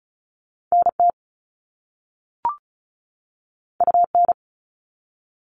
Morse code, heard as a keyed tone, about 33 wpm, 690 Hz. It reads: NT UN
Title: All Cut Numbers - 2-Digit 35wpm